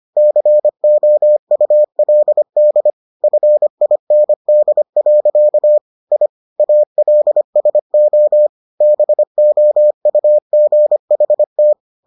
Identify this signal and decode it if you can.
Morse code, heard as a keyed tone, about 25 wpm, 600 Hz. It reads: COULD FIND. I ALSO BOUGHT